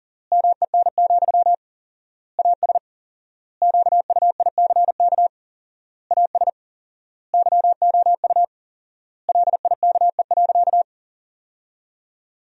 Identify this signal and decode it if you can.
Morse code, heard as a keyed tone, about 40 wpm, 700 Hz. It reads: MEN, AS QUICK AS YOU LIKE.
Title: 4x Word-Spacing - Sentences from Top 500 Words - 40wpm